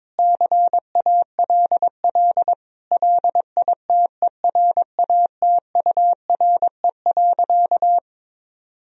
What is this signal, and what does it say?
Morse code, heard as a keyed tone, about 21 wpm, 700 Hz. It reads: CALL LITERATURE.
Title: Sentences from Top 1000 Words - Review Part 3 - 22wpm